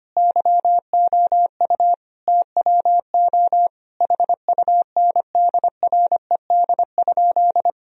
Morse code, heard as a keyed tone, about 25 wpm, 700 Hz. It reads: YOU TWO HUNDRED?
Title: Sentences from Top 1000 Words - Review Part 2 - 25wpm